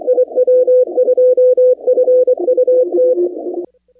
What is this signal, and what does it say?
Morse code, heard as a keyed tone, about 24 wpm, 500 Hz. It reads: IW2FUT
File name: contests / 2019-cw-wpx-kt5j-k5pi- -2330 0-25